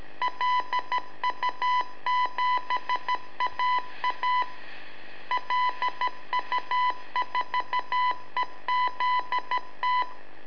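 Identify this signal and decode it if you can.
Morse code, heard as a keyed tone, about 12 wpm, 1000 Hz. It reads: LU7AA LU4EZT
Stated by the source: Telemetry in Morse from LU7AA and LU4EZ, AMSAT and Turma Mar del Plata, sent from a balloon.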